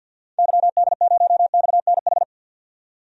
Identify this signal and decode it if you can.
Morse code, heard as a keyed tone, about 52 wpm, 700 Hz. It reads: YD0XNH
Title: DXCC-100 Call Signs - 6 Characters - 50wpm